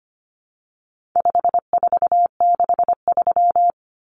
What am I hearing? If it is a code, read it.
Morse code, heard as a keyed tone, about 25 wpm, 700 Hz. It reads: 5463
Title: Numbers - 4-Digit 25wpm